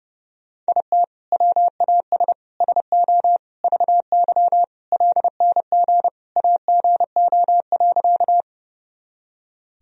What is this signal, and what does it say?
Morse code, heard as a keyed tone, about 30 wpm, 700 Hz. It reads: IT WAS SO VY LNG AGO.
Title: Sentences from Top 500 Words and Abbreviations 30wpm